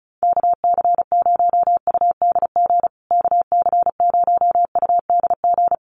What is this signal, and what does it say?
Morse code, heard as a keyed tone, about 34 wpm, 700 Hz. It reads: KC0UDG KC0UDG
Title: USA Call Signs Repeated 3x 35wpm